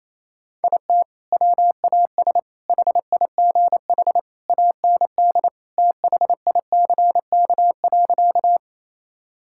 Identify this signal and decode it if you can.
Morse code, heard as a keyed tone, about 28 wpm, 700 Hz. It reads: IT WAS HIGH AND THICK.
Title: Sentences from Top 1600 Words 28wpm